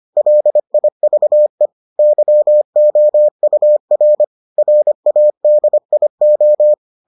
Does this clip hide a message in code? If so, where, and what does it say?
Yes: Morse code from 0.2 to 6.7 s, heard as a keyed tone, about 25 wpm, 600 Hz. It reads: LIVE YOUR RADIO